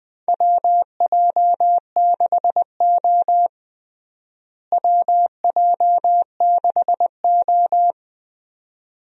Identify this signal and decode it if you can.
Morse code, heard as a keyed tone, about 20 wpm, 700 Hz. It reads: WJ6O WJ6O
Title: USA Call Signs - 4 Characters - Repeated 2x - 20wpm